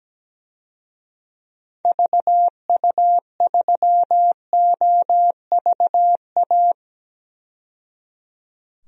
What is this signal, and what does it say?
Morse code, heard as a keyed tone, about 17 wpm, 700 Hz. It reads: VU3OVA
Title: DXCC-100 Call Signs - 6 Characters - 17wpm